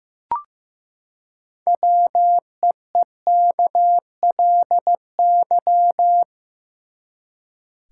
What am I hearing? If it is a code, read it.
Morse code, heard as a keyed tone, about 15 wpm, 700 Hz. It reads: WEEKLY
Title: Six Letter Words - 15wpm